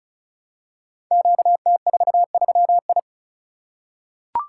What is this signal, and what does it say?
Morse code, heard as a keyed tone, about 36 wpm, 700 Hz. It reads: QT43I
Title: Code Groups - 5 Letter:Numbers - 35wpm